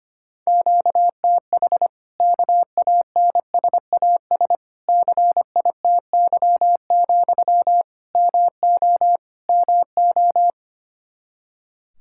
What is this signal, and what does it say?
Morse code, heard as a keyed tone, about 25 wpm, 700 Hz. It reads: QTH KANSAS CITY, MO MO
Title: Practice QSO - Exchange 1 - Beginning - 25wpm